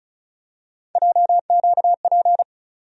Morse code, heard as a keyed tone, about 36 wpm, 700 Hz. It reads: JQP
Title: USA Call Sign Postfixes - 3 Letters 35wpm